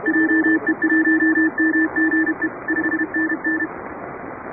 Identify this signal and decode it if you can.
Morse code, heard as a keyed tone, about 31 wpm, 310 Hz, with noise. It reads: JE1MGE 5NN
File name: chatham / 2350-12CW